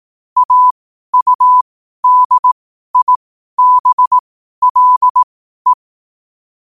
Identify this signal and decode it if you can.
Morse code, heard as a keyed tone, about 18 wpm, spaced as for 12 wpm, 1000 Hz. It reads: AUDIBLE
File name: qotdfiles / 12wpm